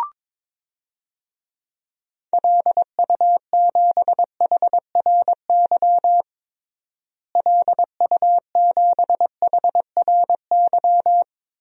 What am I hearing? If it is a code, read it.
Morse code, heard as a keyed tone, about 22 wpm, 700 Hz. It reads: LU7HRY LU7HRY
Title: DXCC-100 Call Signs - 6 Characters - Repeated 3x - 22wpm